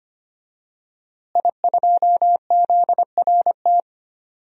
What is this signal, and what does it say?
Morse code, heard as a keyed tone, about 25 wpm, 700 Hz. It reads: I2ZRT
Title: DXCC-100 Call Signs - 5 Characters - Repeated 3x - 25wpm